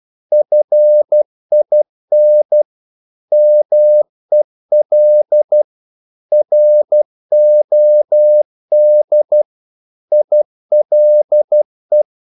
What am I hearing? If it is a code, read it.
Morse code, heard as a keyed tone, about 12 wpm, 600 Hz. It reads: FIN MEL ROD ILE
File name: Korte ord DK 12wpm